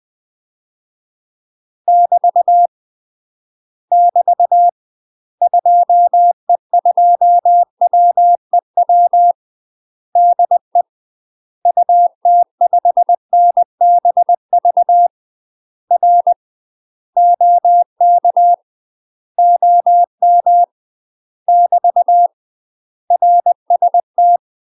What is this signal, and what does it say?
Morse code, heard as a keyed tone, about 20 wpm, 700 Hz. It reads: <BT> <BT> 2E2WEW DE UT5NBV R OK OM <BT> RST
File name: UK MASTER QSOs TRAINING / 20WPM / UK MASTER QSOs 0052